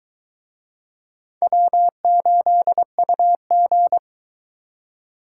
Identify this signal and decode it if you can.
Morse code, heard as a keyed tone, about 23 wpm, 700 Hz. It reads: W8UG